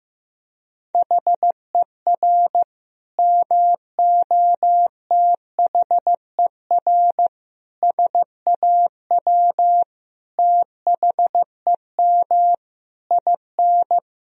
Morse code, heard as a keyed tone, about 15 wpm, 700 Hz. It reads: HER MOTHER SAW THEM IN
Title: Sentences from Top 1000 Words - Review Part 3 - 15wpm